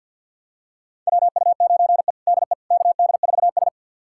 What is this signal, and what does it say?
Morse code, heard as a keyed tone, about 47 wpm, 700 Hz. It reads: WU9E DE KD4S
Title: USA Callsign DE Callsign - 4 Characters - 50wpm